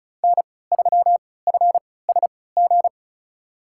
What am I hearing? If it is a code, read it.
Morse code, heard as a keyed tone, about 34 wpm, spaced as for 20 wpm, 700 Hz. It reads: N3FSG
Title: USA Call Signs - 5 Characters - 3x Character Spacing - 35wpm